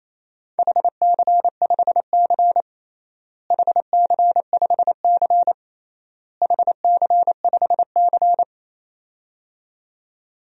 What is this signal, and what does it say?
Morse code, heard as a keyed tone, about 28 wpm, 700 Hz. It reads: HC5C HC5C HC5C
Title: DXCC-100 Call Signs - 4 Characters - Repeated 3x - 28wpm